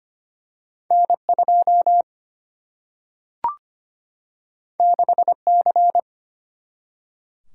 Morse code, heard as a keyed tone, about 25 wpm, 700 Hz. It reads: N2 6C
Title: Single-Letter Single-Number 25wpm